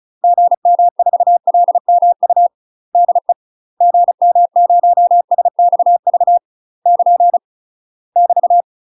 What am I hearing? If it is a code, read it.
Morse code, heard as a keyed tone, about 35 wpm, 700 Hz. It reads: GM4LMU DE GM0SXV <KN> <BT>
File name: UK MASTER QSOs TRAINING / 35WPM / UK MASTER QSOs 0034